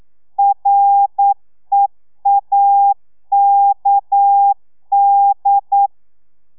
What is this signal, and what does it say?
Morse code, heard as a keyed tone, about 9 wpm, 800 Hz. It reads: REAKD
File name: telegrafia / i2tfj / Lez6 40